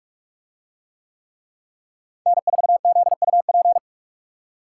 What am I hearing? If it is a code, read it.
Morse code, heard as a keyed tone, about 44 wpm, 700 Hz. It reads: N4ZUP